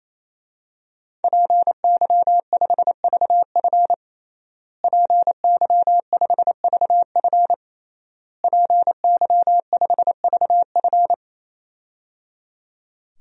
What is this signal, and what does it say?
Morse code, heard as a keyed tone, about 28 wpm, 700 Hz. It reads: PY5VF PY5VF PY5VF
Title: DXCC-100 Call Signs - 5 Characters - Repeated 3x - 28wpm